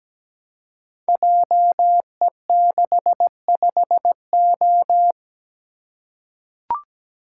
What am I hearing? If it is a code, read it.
Morse code, heard as a keyed tone, about 17 wpm, 700 Hz. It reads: JE65O
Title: Code Groups - 5 Letter:Numbers - 17wpm